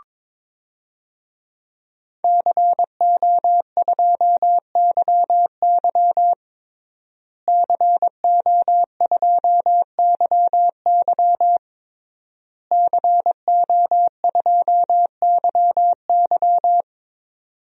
Morse code, heard as a keyed tone, about 22 wpm, 700 Hz. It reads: CO2YY CO2YY CO2YY
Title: DXCC-100 Call Signs - 5 Characters - Repeated 3x - 22wpm